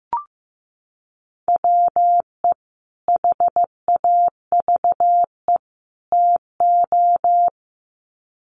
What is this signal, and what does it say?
Morse code, heard as a keyed tone, about 15 wpm, 700 Hz. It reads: WE HAVE TO
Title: Sets of 3 Words 15wpm